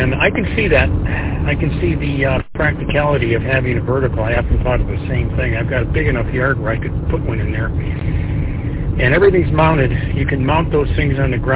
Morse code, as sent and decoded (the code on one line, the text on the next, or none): none